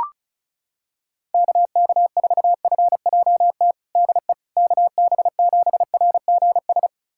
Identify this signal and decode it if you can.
Morse code, heard as a keyed tone, about 34 wpm, 700 Hz. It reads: KK4FJT DE KB7RGS
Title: USA Callsign DE Callsign - 6 Characters - 35wpm